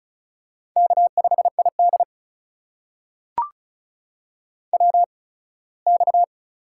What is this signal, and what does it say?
Morse code, heard as a keyed tone, about 36 wpm, spaced as for 21 wpm, 700 Hz. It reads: K5ID WX
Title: USA Call Signs - 4 Characters - 8x Character Spacing - 35wpm